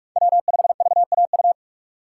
Morse code, heard as a keyed tone, about 47 wpm, 700 Hz. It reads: W5VAU
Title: USA Call Signs - 2x Character Spacing - 45wpm